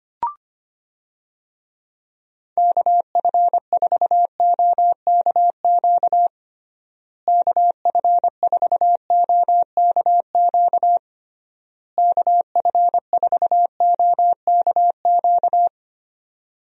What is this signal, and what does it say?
Morse code, heard as a keyed tone, about 25 wpm, 700 Hz. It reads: KF4OKQ KF4OKQ KF4OKQ
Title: USA Call Signs - 6 Characters - Repeated 3x - 25wpm